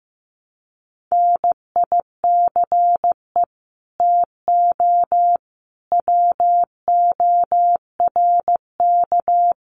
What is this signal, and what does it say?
Morse code, heard as a keyed tone, about 15 wpm, 700 Hz. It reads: NICE TO WORK
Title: Common QSO Phrases 15wpm